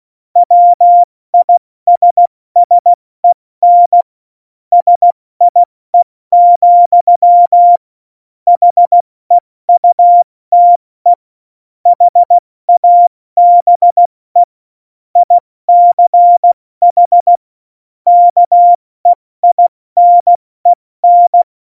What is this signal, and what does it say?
Morse code, heard as a keyed tone, about 16 wpm, 700 Hz. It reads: WISSEN SIE, HEUTE HABE ICH KEINEN
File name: Morsekurs / Witze / Witz 16. April 2019, 70 F80